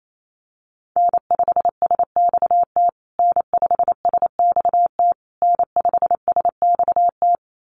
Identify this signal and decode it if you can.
Morse code, heard as a keyed tone, about 28 wpm, 700 Hz. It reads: N5SXT N5SXT N5SXT
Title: USA Call Signs Repeated 3x 28wpm